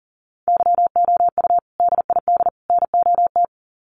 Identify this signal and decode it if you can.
Morse code, heard as a keyed tone, about 40 wpm, 700 Hz. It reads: YOU DID NOT
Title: Sets of 3 Words 40wpm